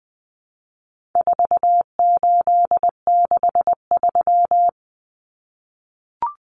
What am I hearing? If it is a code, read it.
Morse code, heard as a keyed tone, about 19 wpm, 700 Hz. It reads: 4863
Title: Common Cut Numbers - 4-Digit 20wpm